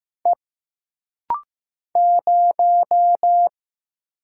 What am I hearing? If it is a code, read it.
Morse code, heard as a keyed tone, about 15 wpm, 700 Hz. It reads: E 0
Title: Single Letter-Number Alt - Rapid-Fire 15wpm